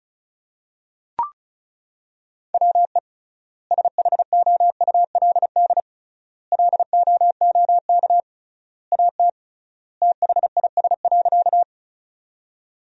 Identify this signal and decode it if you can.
Morse code, heard as a keyed tone, about 36 wpm, 700 Hz. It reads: WE SHOULD LOOK AT THIS.
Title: 3x Word-Spacing - Sentences from Top 300 Words - 35wpm